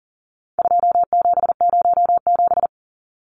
Morse code, heard as a keyed tone, about 40 wpm, 700 Hz. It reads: 2707
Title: Numbers - 4-Digit 40wpm